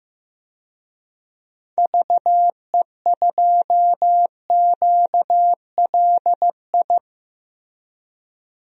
Morse code, heard as a keyed tone, about 15 wpm, 700 Hz. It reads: VE2QLI
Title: DXCC-100 Call Signs - 6 Characters - Repeated 3x - 15wpm